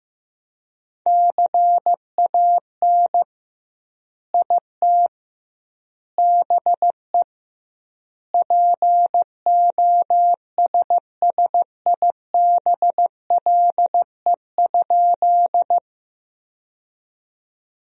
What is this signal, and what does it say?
Morse code, heard as a keyed tone, about 15 wpm, 700 Hz. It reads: CAN IT BE POSSIBLE?
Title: 2x Word-Spacing - Sentences from Top 500 Words - 15wpm